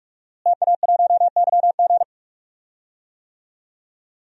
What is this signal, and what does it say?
Morse code, heard as a keyed tone, about 46 wpm, 700 Hz. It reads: TA1YG
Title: DXCC-100 Call Signs - 5 Characters - Repeated 3x - 45wpm